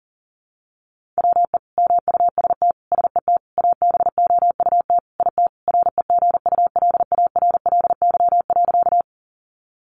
Morse code, heard as a keyed tone, about 40 wpm, 700 Hz. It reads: WE MUST SET ABOUT IT REGULARLY.
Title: Sentences from Top 1400 Words 40wpm